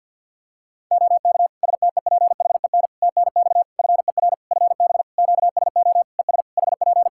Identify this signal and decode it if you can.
Morse code, heard as a keyed tone, about 50 wpm, 700 Hz. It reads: OK STEPHEN TNX FER FB QSO ES HP
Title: Practice QSO - Exchange 4 - Ending - 50wpm